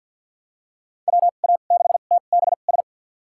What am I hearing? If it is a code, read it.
Morse code, heard as a keyed tone, about 52 wpm, spaced as for 36 wpm, 700 Hz. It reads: WA6TBS
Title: USA Call Signs - 2x Character Spacing - 50wpm